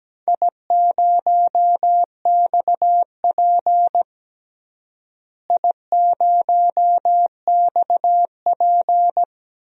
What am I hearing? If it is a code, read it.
Morse code, heard as a keyed tone, about 17 wpm, 700 Hz. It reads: I0XP I0XP
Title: DXCC-100 Call Signs - 4 Characters - Repeated 3x - 17wpm